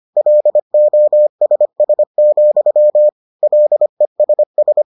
Morse code, heard as a keyed tone, about 25 wpm, 600 Hz. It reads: LOSS, LESS